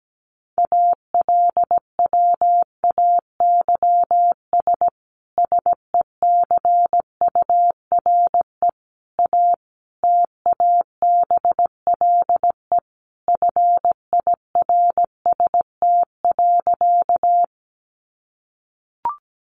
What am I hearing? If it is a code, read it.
Morse code, heard as a keyed tone, about 17 wpm, 700 Hz. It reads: ALWAYS SECURE A TABLE FIRST.
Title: Sentences from Top 1200 Words 17wpm